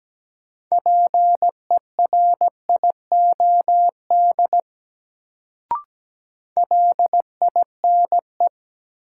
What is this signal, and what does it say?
Morse code, heard as a keyed tone, about 17 wpm, 700 Hz. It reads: PERIOD LINE
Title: Sets of 5 Words - New Word Drill - 17wpm